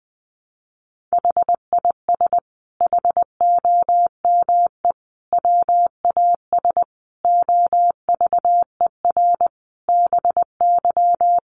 Morse code, heard as a keyed tone, about 20 wpm, 700 Hz. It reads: HIS HOME WAS OVER BY
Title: Sentences from Top 1500 Words 20wpm